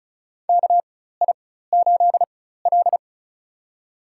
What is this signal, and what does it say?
Morse code, heard as a keyed tone, about 36 wpm, spaced as for 16 wpm, 700 Hz. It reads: KI8L
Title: USA Call Signs - 4 Characters - 4x Character Spacing - 35wpm